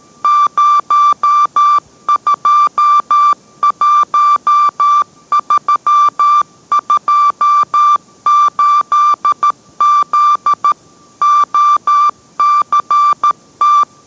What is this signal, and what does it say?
Morse code, heard as a keyed tone, about 13 wpm, 1180 Hz. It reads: 021328Z OCT